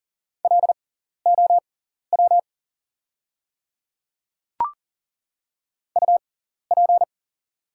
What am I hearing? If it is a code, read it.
Morse code, heard as a keyed tone, about 40 wpm, spaced as for 14 wpm, 700 Hz. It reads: LOW UP
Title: 6x Character Spacing - Top 200 Words - 40wpm